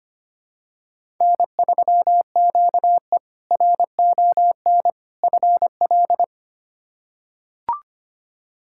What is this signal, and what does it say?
Morse code, heard as a keyed tone, about 25 wpm, 700 Hz. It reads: N3QE RON FL
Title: K1USN SST QSO Practice - 25wpm